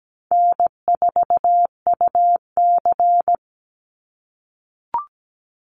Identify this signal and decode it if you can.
Morse code, heard as a keyed tone, about 17 wpm, 700 Hz. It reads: N4UC
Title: Call Signs - 17wpm